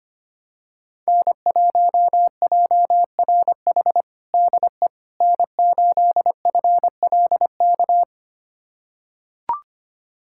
Morse code, heard as a keyed tone, about 25 wpm, 700 Hz. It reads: N1JRH DE N8FLK
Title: USA Callsign DE Callsign - 5 Characters - 25wpm